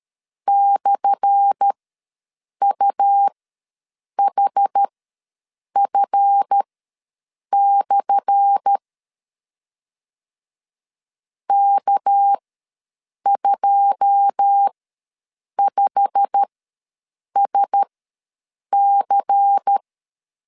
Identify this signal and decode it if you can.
Morse code, heard as a keyed tone, about 13 wpm, spaced as for 7 wpm, 800 Hz. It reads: /UHF/ K25SC
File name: LEZIONI / 10 SIGNS / 10A SLASH ws 12wpm 3443